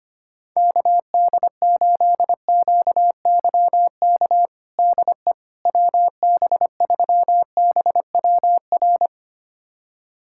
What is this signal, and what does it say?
Morse code, heard as a keyed tone, about 25 wpm, 700 Hz. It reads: KD8QYK DE WB3BWR
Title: USA Callsign DE Callsign 25wpm